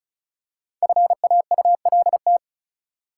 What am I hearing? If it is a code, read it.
Morse code, heard as a keyed tone, about 36 wpm, 700 Hz. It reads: FAULT
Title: Five Letter Words - 35wpm